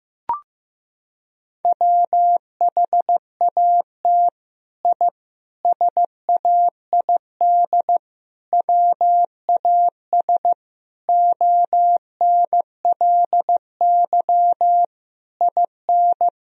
Morse code, heard as a keyed tone, about 15 wpm, 700 Hz. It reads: WHAT I SAID WAS ONLY IN
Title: Sentences from Top 1000 Words - Review Part 4 - 15wpm